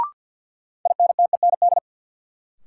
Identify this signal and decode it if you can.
Morse code, heard as a keyed tone, about 48 wpm, 700 Hz. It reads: INTEND
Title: Sentences from Top 1600 Words - New Word Drill Alt - 50wpm